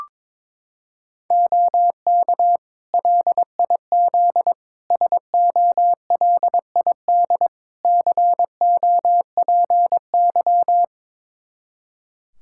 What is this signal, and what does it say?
Morse code, heard as a keyed tone, about 22 wpm, 700 Hz. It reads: OK LIZ SOLID COPY
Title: K1USN SST QSO Practice 22wpm